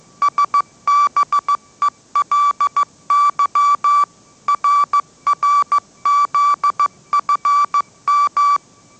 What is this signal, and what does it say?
Morse code, heard as a keyed tone, about 15 wpm, 1160 Hz. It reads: SBELY RRZFM